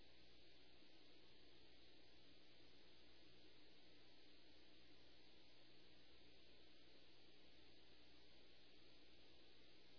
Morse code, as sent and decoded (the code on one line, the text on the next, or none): none